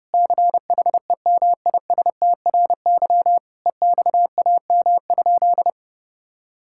Morse code, heard as a keyed tone, about 30 wpm, 700 Hz. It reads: CHEMISTRY EXAM?
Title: Sentences from Top 1000 Words - Review Part 6 - 30wpm